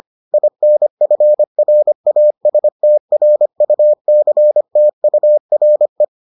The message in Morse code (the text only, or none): INFRASTRUCTURE